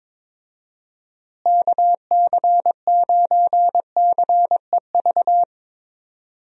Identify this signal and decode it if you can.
Morse code, heard as a keyed tone, about 22 wpm, 700 Hz. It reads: KC9CEV
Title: USA Call Signs - 6 Characters - Repeated 2x - 22wpm